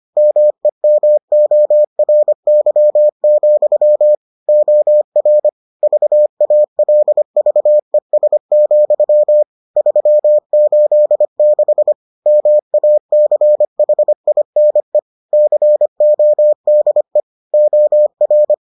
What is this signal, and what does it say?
Morse code, heard as a keyed tone, about 25 wpm, 600 Hz. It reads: MEMORY, OR VALVES, 386 MACHINE CODE OR